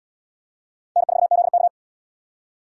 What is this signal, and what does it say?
Morse code, heard as a keyed tone, about 67 wpm, 700 Hz. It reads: N4ZK